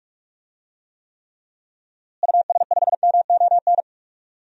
Morse code, heard as a keyed tone, about 48 wpm, 700 Hz. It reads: US5MON
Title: DXCC-100 Call Signs - 6 Characters - Repeated 3x - 45wpm